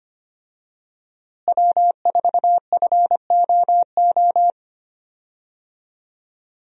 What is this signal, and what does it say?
Morse code, heard as a keyed tone, about 25 wpm, 700 Hz. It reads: W4FOO
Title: USA Call Signs - 5 Characters - Repeated 3x - 25wpm